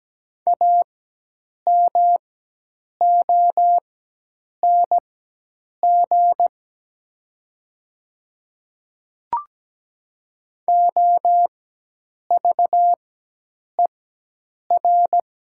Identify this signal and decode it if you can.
Morse code, heard as a keyed tone, about 17 wpm, spaced as for 8 wpm, 700 Hz. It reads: AMONG OVER
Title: 4x Character Spacing - Top 500 Words - 17wpm